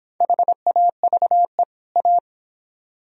Morse code, heard as a keyed tone, about 30 wpm, 700 Hz. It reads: HAVE A